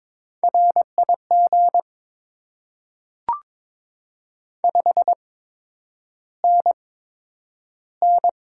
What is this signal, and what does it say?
Morse code, heard as a keyed tone, about 22 wpm, spaced as for 9 wpm, 700 Hz. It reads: RIG 5NN
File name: Top 20 QSO Elements - 8x Character Spacing - 22wpm